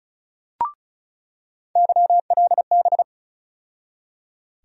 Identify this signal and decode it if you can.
Morse code, heard as a keyed tone, about 36 wpm, 700 Hz. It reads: YLB
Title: USA Call Sign Postfixes - 3 Letters 35wpm